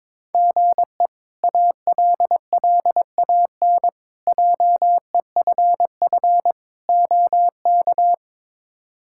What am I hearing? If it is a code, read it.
Morse code, heard as a keyed tone, about 21 wpm, 700 Hz. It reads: GE ALLAN JEFF OK
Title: K1USN SST QSO Practice - 22wpm